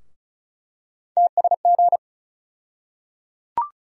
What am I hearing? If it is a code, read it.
Morse code, heard as a keyed tone, about 34 wpm, 700 Hz. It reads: TSG
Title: USA Call Sign Postfixes - 3 Letters 35wpm